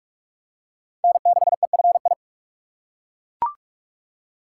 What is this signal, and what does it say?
Morse code, heard as a keyed tone, about 41 wpm, 700 Hz. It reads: N6EFI
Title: USA Call Signs - 5 Characters - 4x Character Spacing - 45wpm